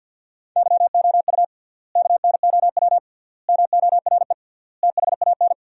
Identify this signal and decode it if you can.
Morse code, heard as a keyed tone, about 50 wpm, 700 Hz. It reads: YOU KNOW MORE THAN
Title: 3x Word-Spacing - Sentences from Top 100 Words - 50wpm